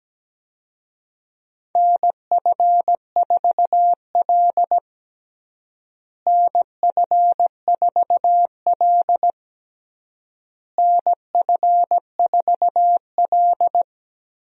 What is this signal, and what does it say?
Morse code, heard as a keyed tone, about 17 wpm, 700 Hz. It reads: NF4L NF4L NF4L
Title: USA Call Signs - 4 Characters - Repeated 3x - 17wpm